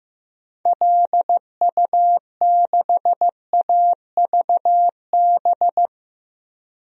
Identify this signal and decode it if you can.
Morse code, heard as a keyed tone, about 15 wpm, 700 Hz. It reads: LU6AVB
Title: DXCC-100 Call Signs - 6 Characters - Repeated 3x - 15wpm